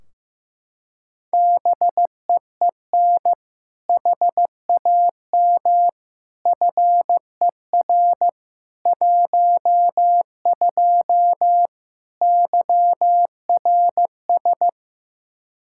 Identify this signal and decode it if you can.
Morse code, heard as a keyed tone, about 15 wpm, 700 Hz. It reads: BEEN HAM FER 12 YRS
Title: Practice QSO - Exchange 3 - More Info - 15wpm